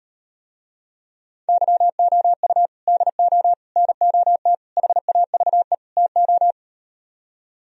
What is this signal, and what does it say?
Morse code, heard as a keyed tone, about 38 wpm, 700 Hz. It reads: YOU DO NOT HAVE TO